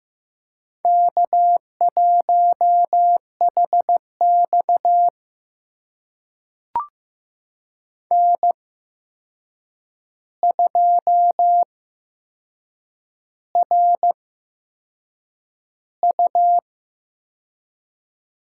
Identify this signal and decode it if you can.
Morse code, heard as a keyed tone, about 15 wpm, spaced as for 6 wpm, 700 Hz. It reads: K1HX N2RU
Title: USA Call Signs - 8x Character Spacing - 15wpm